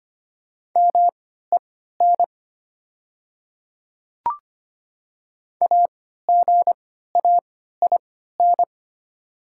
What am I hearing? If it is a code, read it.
Morse code, heard as a keyed tone, about 25 wpm, spaced as for 14 wpm, 700 Hz. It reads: MEN AGAIN
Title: 3x Character Spacing - Top 300 Words - 25wpm